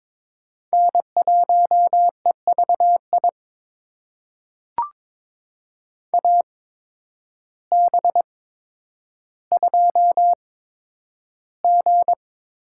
Morse code, heard as a keyed tone, about 22 wpm, spaced as for 10 wpm, 700 Hz. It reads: N1EVI AB2G
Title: USA Call Signs - 5 Characters - 8x Character Spacing - 22wpm